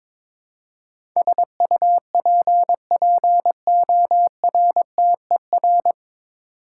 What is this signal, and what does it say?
Morse code, heard as a keyed tone, about 22 wpm, 700 Hz. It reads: SUPPORTER
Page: Morse Code Ninja practice 22wpm